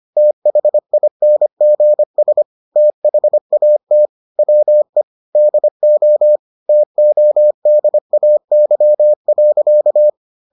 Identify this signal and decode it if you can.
Morse code, heard as a keyed tone, about 25 wpm, 600 Hz. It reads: THINGS THAT WE DO TODAY.